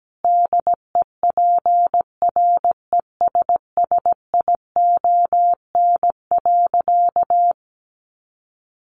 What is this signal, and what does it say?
Morse code, heard as a keyed tone, about 17 wpm, 700 Hz. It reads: DEPRESSION.
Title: Sentences from Top 1300 Words 17wpm